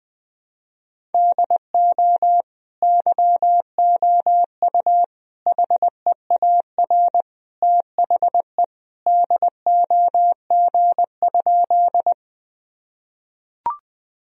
Morse code, heard as a keyed tone, about 21 wpm, 700 Hz. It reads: DO YOU HEAR THE DOG?
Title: Sentences from Top 1000 Words - Review Part 2 - 20wpm